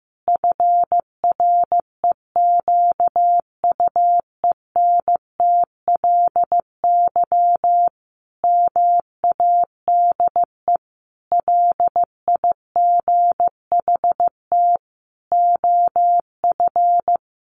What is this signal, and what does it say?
Morse code, heard as a keyed tone, about 15 wpm, 700 Hz. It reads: FREQUENTLY MADE LIGHT OF
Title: Sentences from Top 1300 Words 15wpm